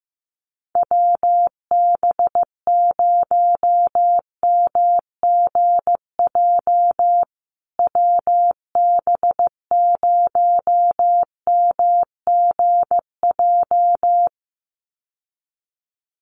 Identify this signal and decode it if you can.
Morse code, heard as a keyed tone, about 15 wpm, 700 Hz. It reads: WB0MGJ WB0MGJ
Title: USA Call Signs Repeated 2x 15wpm